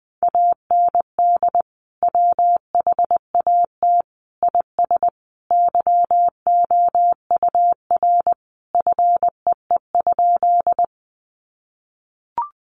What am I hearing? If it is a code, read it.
Morse code, heard as a keyed tone, about 20 wpm, 700 Hz. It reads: AND WHAT IS YOUR FEE?
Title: Sentences from Top 1500 Words 20wpm